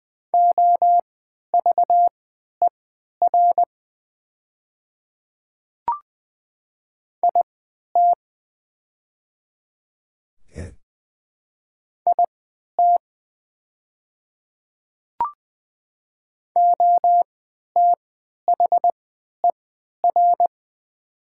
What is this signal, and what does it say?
Morse code, heard as a keyed tone, about 20 wpm, spaced as for 11 wpm, 690 Hz. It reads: OVER IT IT OTHER
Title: 3x Character Spacing - Top 100 Words - 20wpm